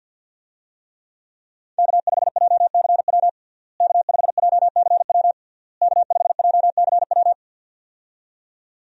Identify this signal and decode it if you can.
Morse code, heard as a keyed tone, about 52 wpm, 690 Hz. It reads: K5JCW K5JCW K5JCW
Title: USA Call Signs - 5 Characters - Repeated 3x - 50wpm